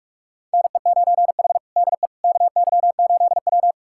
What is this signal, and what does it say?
Morse code, heard as a keyed tone, about 41 wpm, 700 Hz. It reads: NE9H DE KY8W
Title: USA Callsign DE Callsign - 4 Characters - 45wpm